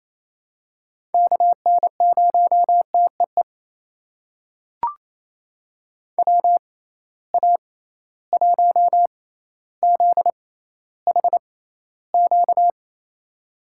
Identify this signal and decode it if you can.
Morse code, heard as a keyed tone, about 28 wpm, spaced as for 14 wpm, 700 Hz. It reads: KN0TEE WA1ZHQ
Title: USA Call Signs - 6 Characters - 6x Character Spacing - 28wpm